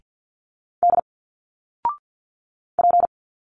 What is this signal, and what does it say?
Morse code, heard as a keyed tone, about 50 wpm, 700 Hz. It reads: D P
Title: Single Letters 50wpm